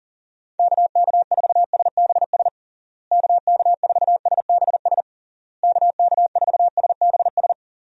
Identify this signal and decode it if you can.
Morse code, heard as a keyed tone, about 40 wpm, 700 Hz. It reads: KK4SBS KK4SBS KK4SBS
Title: USA Call Signs - 6 Characters - Repeated 3x - 40wpm